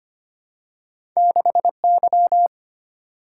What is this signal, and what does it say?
Morse code, heard as a keyed tone, about 25 wpm, 700 Hz. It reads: 6Y
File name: Two Letter-Number Combinations - 25wpm